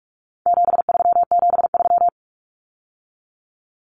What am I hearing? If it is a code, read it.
Morse code, heard as a keyed tone, about 44 wpm, 700 Hz. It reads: 7373
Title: Common Cut Numbers - 4-Digit 45wpm